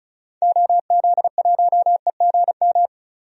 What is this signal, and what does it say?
Morse code, heard as a keyed tone, about 34 wpm, 700 Hz. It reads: OZ1EGM
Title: DXCC-100 Call Signs - 6 Characters - Repeated 3x - 35wpm